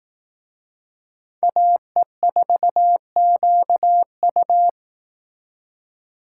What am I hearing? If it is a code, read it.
Morse code, heard as a keyed tone, about 18 wpm, 700 Hz. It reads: AE4QU
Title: Speed Racing - USA Call Signs 15wpm